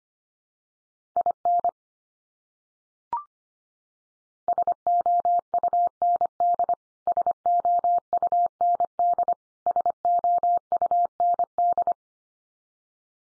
Morse code, heard as a keyed tone, about 25 wpm, 700 Hz. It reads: IN SOUND SOUND SOUND
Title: Top 100 Words 25wpm x3